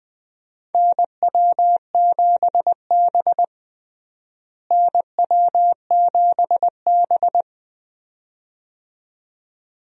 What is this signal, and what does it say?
Morse code, heard as a keyed tone, about 20 wpm, 700 Hz. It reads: NW7B NW7B
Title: USA Call Signs - 4 Characters - Repeated 2x - 20wpm